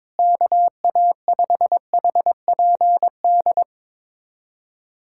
Morse code, heard as a keyed tone, about 22 wpm, 700 Hz. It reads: KA5HPD
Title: USA Callsign DE Callsign - 6 Characters - 22wpm